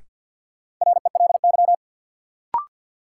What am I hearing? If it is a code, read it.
Morse code, heard as a keyed tone, about 52 wpm, 700 Hz. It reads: RELY
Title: Sentences from Top 1200 Words - New Word Drill Alt - 50wpm